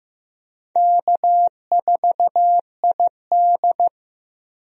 Morse code, heard as a keyed tone, about 16 wpm, 700 Hz. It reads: K4ID
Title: USA Call Signs - 4 Characters - Repeated 2x - 15wpm